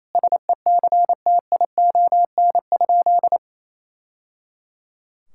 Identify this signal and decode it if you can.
Morse code, heard as a keyed tone, about 28 wpm, 700 Hz. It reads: SECTION?
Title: Sentences from Top 1000 Words - Review Part 5 - 28wpm